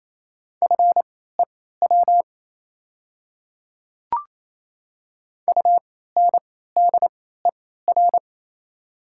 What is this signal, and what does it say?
Morse code, heard as a keyed tone, about 28 wpm, spaced as for 16 wpm, 690 Hz. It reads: FEW UNDER
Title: 3x Character Spacing - Top 300 Words - 28wpm